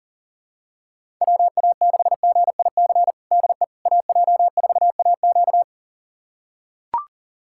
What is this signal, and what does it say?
Morse code, heard as a keyed tone, about 40 wpm, 700 Hz. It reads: WA6GIC DE AJ4AQ
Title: USA Callsign DE Callsign 40wpm